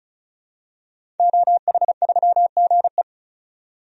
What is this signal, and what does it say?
Morse code, heard as a keyed tone, about 36 wpm, 700 Hz. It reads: OH3GE
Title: DXCC-100 Call Signs - 5 Characters - 35wpm